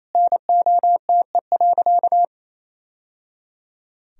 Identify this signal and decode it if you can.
Morse code, heard as a keyed tone, about 28 wpm, 700 Hz. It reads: NOTE.
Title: Sentences from Top 1000 Words - Review Part 2 - 28wpm